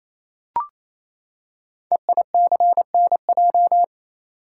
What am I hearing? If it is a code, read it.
Morse code, heard as a keyed tone, about 29 wpm, 700 Hz. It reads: EICNJ
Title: Code Groups - 5 Letters - 28wpm